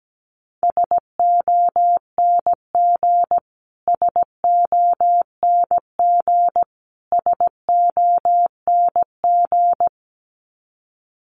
Morse code, heard as a keyed tone, about 17 wpm, 700 Hz. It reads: SONG SONG SONG
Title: Top 500 Words Repeated 17wpm